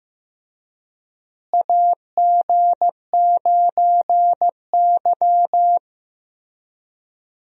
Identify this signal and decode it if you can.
Morse code, heard as a keyed tone, about 15 wpm, 700 Hz. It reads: AG9Y
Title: USA Call Signs - 4 Characters - Repeated 2x - 15wpm